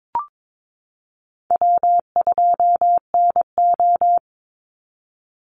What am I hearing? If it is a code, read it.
Morse code, heard as a keyed tone, about 22 wpm, 700 Hz. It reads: W2NO
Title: Easy Call Signs - 22wpm